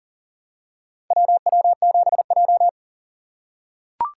Morse code, heard as a keyed tone, about 40 wpm, 700 Hz. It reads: WW7J
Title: USA Call Signs - 4x Character Spacing - 40wpm